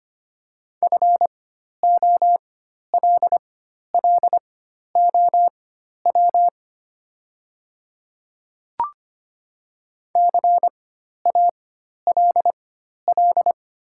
Morse code, heard as a keyed tone, about 25 wpm, spaced as for 12 wpm, 700 Hz. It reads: FOLLOW CALL
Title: 4x Character Spacing - Top 500 Words - 25wpm